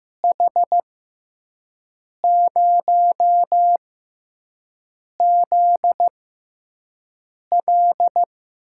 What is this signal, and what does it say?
Morse code, heard as a keyed tone, about 15 wpm, spaced as for 5 wpm, 700 Hz. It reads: H0ZL
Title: USA Call Signs - 5 Characters - 6x Character Spacing - 15wpm